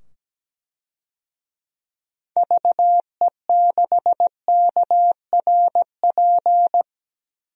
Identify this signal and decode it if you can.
Morse code, heard as a keyed tone, about 17 wpm, 700 Hz. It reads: VE6KRP
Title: DXCC-100 Call Signs - 6 Characters - 17wpm